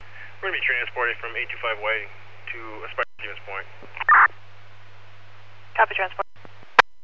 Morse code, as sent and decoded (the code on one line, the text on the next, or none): none